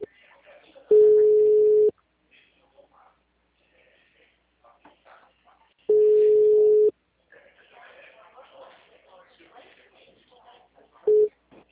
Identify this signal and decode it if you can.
Ringback tone (Europe, 425 Hz), with speech.
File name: out- -2001- - - .262